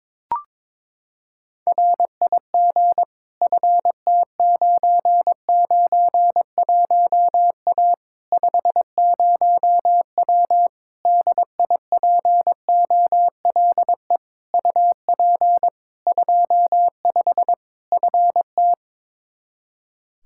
Morse code, heard as a keyed tone, about 22 wpm, 700 Hz. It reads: RIG FT991A 50W DIPOLE UP 25 FT
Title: Practice QSO - Exchange 2 - Info - 22wpm